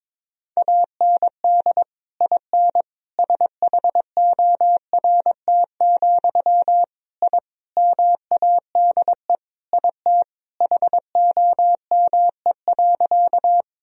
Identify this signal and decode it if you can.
Morse code, heard as a keyed tone, about 22 wpm, 700 Hz. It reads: AND IN SHORT, I MADE IT HOME.
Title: Sentences from Top 1000 Words - Review Part 2 - 22wpm